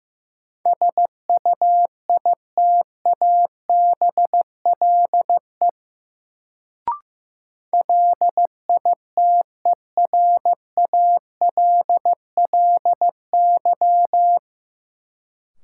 Morse code, heard as a keyed tone, about 15 wpm, 700 Hz. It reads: SUITABLE LITERALLY
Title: Sentences from Top 1400 Words - New Word Drill - 15wpm